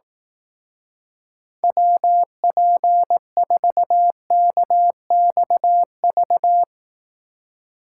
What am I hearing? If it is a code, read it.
Morse code, heard as a keyed tone, about 18 wpm, 700 Hz. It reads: WP4KXV
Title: Speed Racing - USA 2x3 Call Signs 15wpm